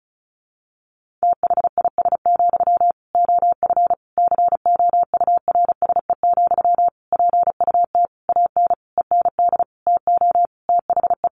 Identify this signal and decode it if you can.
Morse code, heard as a keyed tone, about 34 wpm, 700 Hz. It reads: THIS, OF COURSE, PUT AN END TO THE
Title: Sentences from Top 1400 Words 35wpm